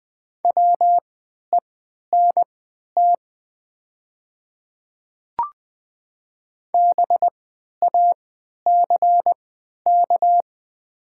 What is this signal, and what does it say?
Morse code, heard as a keyed tone, about 20 wpm, spaced as for 11 wpm, 700 Hz. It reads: WENT BACK
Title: 3x Character Spacing - Top 200 Words - 20wpm